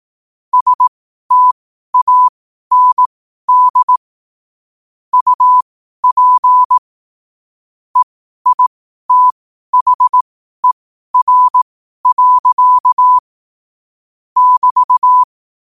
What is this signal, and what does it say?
Morse code, heard as a keyed tone, about 18 wpm, spaced as for 12 wpm, 1000 Hz. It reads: STAND UP EITHER. <BT>
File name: qotdfiles / 12wpm